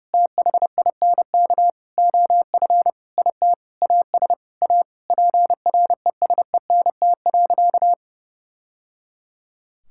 Morse code, heard as a keyed tone, about 30 wpm, 700 Hz. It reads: THINK OF IT AS A PRESENT.
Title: Sentences from Top 1000 Words - Review Part 5 - 30wpm